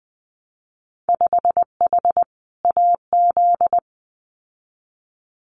Morse code, heard as a keyed tone, about 20 wpm, 700 Hz. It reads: 5H AZ
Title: Winter Field Day Exchanges Encore 20wpm